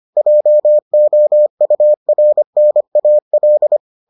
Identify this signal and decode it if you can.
Morse code, heard as a keyed tone, about 25 wpm, 600 Hz. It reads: JOURNAL